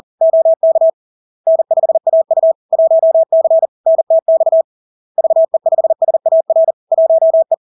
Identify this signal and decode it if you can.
Morse code, heard as a keyed tone, about 40 wpm, 650 Hz. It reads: OK N5AU 1C NTX VE5SAR 1E
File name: LARK-FD-2022-Log-in-CW-40wpm-650-Hz